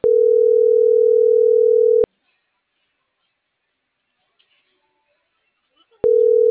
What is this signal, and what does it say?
Ringback tone (North America), 440 + 480 Hz.